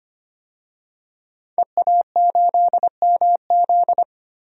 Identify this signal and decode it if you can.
Morse code, heard as a keyed tone, about 25 wpm, 700 Hz. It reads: EA8MZ
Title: DXCC-100 Call Signs - 5 Characters - 25wpm